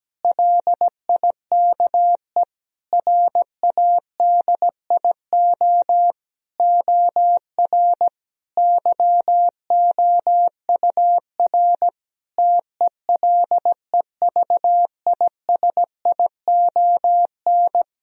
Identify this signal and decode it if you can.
Morse code, heard as a keyed tone, about 17 wpm, 700 Hz. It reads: LIKE RADIO OR YOUR TELEVISION
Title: Sentences from Top 1000 Words - Review Part 4 - 17wpm